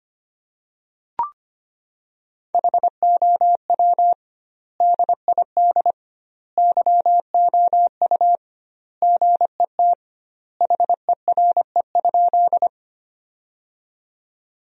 Morse code, heard as a keyed tone, about 25 wpm, 700 Hz. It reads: HOW DID YOU GET HERE?
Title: 2x Word-Spacing - Sentences from Top 200 Words - 25wpm